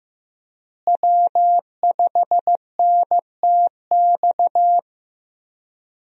Morse code, heard as a keyed tone, about 15 wpm, 700 Hz. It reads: W5NTX
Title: USA Call Signs - 5 Characters - Repeated 2x - 15wpm